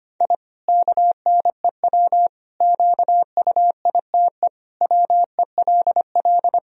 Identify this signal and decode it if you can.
Morse code, heard as a keyed tone, about 25 wpm, 700 Hz. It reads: I KNEW QUITE WELL
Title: Sentences from Top 1000 Words - Review Part 3 - 25wpm